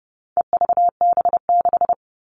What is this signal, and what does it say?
Morse code, heard as a keyed tone, about 30 wpm, 700 Hz. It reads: EVB6
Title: All Cut Numbers - 4-Digit 30wpm